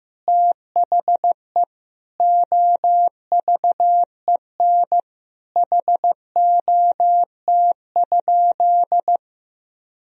Morse code, heard as a keyed tone, about 15 wpm, 700 Hz. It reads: THE OVEN HOT?
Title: Sentences from Top 1000 Words - Review Part 4 - 15wpm